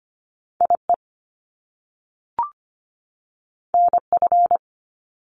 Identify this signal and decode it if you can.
Morse code, heard as a keyed tone, about 25 wpm, 700 Hz. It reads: IE NF
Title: Top Two Letter Combinations in Words 25wpm